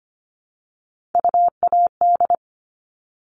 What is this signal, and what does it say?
Morse code, heard as a keyed tone, about 25 wpm, 700 Hz. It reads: UAD
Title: All Cut Numbers - 3-Digit 25wpm